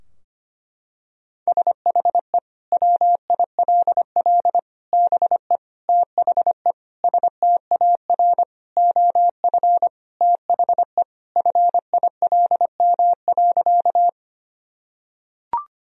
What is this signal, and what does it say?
Morse code, heard as a keyed tone, about 25 wpm, 700 Hz. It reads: SHE WILL BE THE STAR OF THE FILM.
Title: Sentences from Top 1000 Words - Review Part 3 - 25wpm